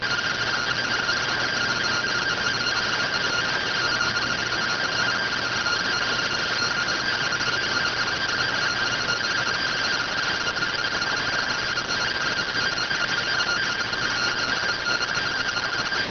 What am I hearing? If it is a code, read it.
Two-tone teleprinter signal, about 170 Hz shift.